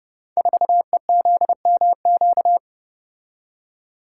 Morse code, heard as a keyed tone, about 30 wpm, 700 Hz. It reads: 4EZMQ
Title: Code Groups - 5 Letter:Numbers - 30wpm